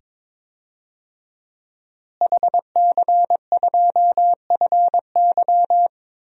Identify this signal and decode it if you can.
Morse code, heard as a keyed tone, about 22 wpm, 700 Hz. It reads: HC2FY
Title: DXCC-100 Call Signs - 5 Characters - Repeated 2x - 22wpm